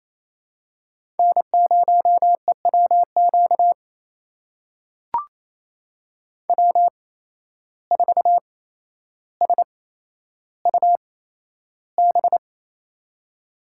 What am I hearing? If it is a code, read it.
Morse code, heard as a keyed tone, about 28 wpm, spaced as for 12 wpm, 700 Hz. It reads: N0EWQ W4SUB
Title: USA Call Signs - 5 Characters - 8x Character Spacing - 28wpm